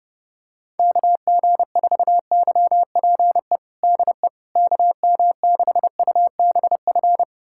Morse code, heard as a keyed tone, about 32 wpm, 690 Hz. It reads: KG4YPE DE KM6UBF
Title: USA Callsign DE Callsign 30wpm